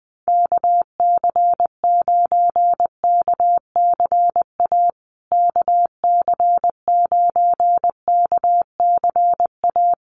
Morse code, heard as a keyed tone, about 19 wpm, 700 Hz. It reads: KC9KCA KC9KCA
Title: USA Call Signs Repeated 3x 20wpm